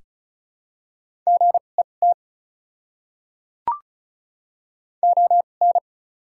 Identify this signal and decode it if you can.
Morse code, heard as a keyed tone, about 36 wpm, spaced as for 25 wpm, 700 Hz. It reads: GET ON
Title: 2x Character Spacing - Top 200 Words - 35wpm